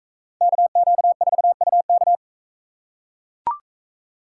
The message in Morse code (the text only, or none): KQ4UK